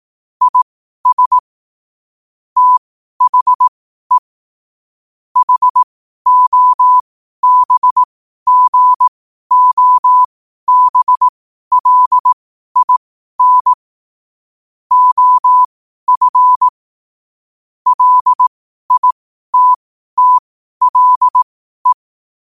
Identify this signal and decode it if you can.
Morse code, heard as a keyed tone, about 18 wpm, spaced as for 12 wpm, 1000 Hz. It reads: IS THE HOBGOBLIN OF LITTLE
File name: qotdfiles / 12wpm